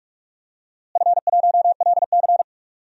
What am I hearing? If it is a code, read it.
Morse code, heard as a keyed tone, about 46 wpm, 700 Hz. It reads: F1LC
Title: DXCC-100 Call Signs - 4 Characters - Repeated 3x - 45wpm